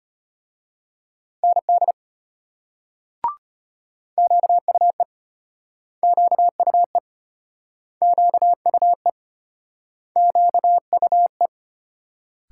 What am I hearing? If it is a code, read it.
Morse code, heard as a keyed tone, about 33 wpm, 700 Hz. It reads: ND QUE QUE QUE QUE